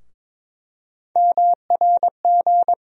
Morse code, heard as a keyed tone, about 22 wpm, 700 Hz. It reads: MRG
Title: USA Call Sign Postfixes - 3 Letters 22wpm